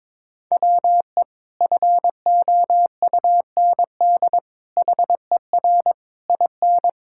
Morse code, heard as a keyed tone, about 22 wpm, 700 Hz. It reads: WE FOUND HER IN
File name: practice / 22wpm / Sentences with 4-Word Phrases - 22wpm